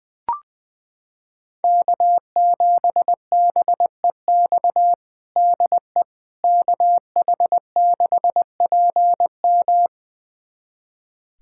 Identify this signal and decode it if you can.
Morse code, heard as a keyed tone, about 20 wpm, 700 Hz. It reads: K7BEX DE KH6PM
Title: USA Callsign DE Callsign - 5 Characters - 20wpm